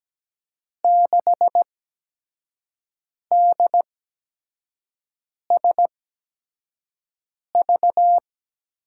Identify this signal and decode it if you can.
Morse code, heard as a keyed tone, about 17 wpm, spaced as for 5 wpm, 700 Hz. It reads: 6DSV
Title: USA Call Signs - 6 Characters - 8x Character Spacing - 17wpm